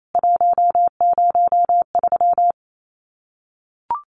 Morse code, heard as a keyed tone, about 28 wpm, 700 Hz. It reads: 103
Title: CWT Exchange 28wpm